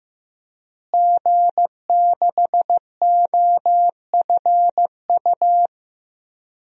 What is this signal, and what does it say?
Morse code, heard as a keyed tone, about 15 wpm, 700 Hz. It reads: G6OFU
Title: DXCC-100 Call Signs - 5 Characters - Repeated 2x - 15wpm